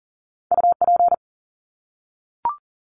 Morse code, heard as a keyed tone, about 40 wpm, 700 Hz. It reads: UP
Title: Top 300 Words 40wpm